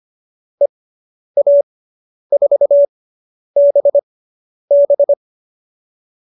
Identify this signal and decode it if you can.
Morse code, heard as a keyed tone, about 25 wpm, spaced as for 10 wpm, 570 Hz. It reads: EA4BB